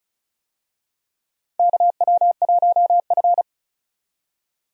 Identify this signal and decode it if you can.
Morse code, heard as a keyed tone, about 36 wpm, 700 Hz. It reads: KW1F
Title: USA Call Signs - 4 Characters - 35wpm